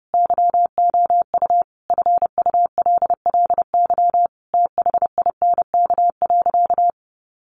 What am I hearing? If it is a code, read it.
Morse code, heard as a keyed tone, about 30 wpm, 700 Hz. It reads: YOU FULLY THINK.
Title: Sentences from Top 1300 Words 30wpm